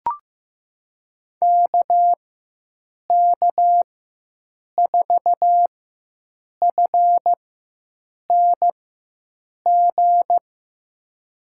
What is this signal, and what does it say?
Morse code, heard as a keyed tone, about 15 wpm, spaced as for 7 wpm, 700 Hz. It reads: KK4FNG
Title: USA Call Signs - 6 Characters - 4x Character Spacing - 15wpm